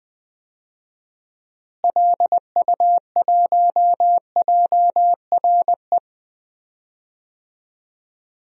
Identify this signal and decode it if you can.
Morse code, heard as a keyed tone, about 20 wpm, 700 Hz. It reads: LU1JRE
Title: DXCC-100 Call Signs - 6 Characters - Repeated 3x - 20wpm